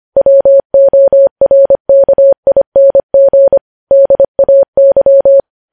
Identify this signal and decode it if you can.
Morse code, heard as a keyed tone, about 25 wpm, 550 Hz. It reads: WORKING DAY
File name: podcast / News headlines-1x-25x25